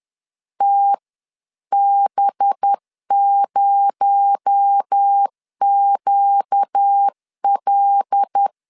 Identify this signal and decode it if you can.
Morse code, heard as a keyed tone, about 10 wpm, 800 Hz. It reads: TB0QL